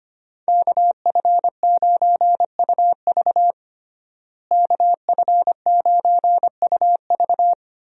Morse code, heard as a keyed tone, about 25 wpm, 700 Hz. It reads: KF9UV KF9UV
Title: USA Call Signs - 5 Characters - Repeated 3x - 25wpm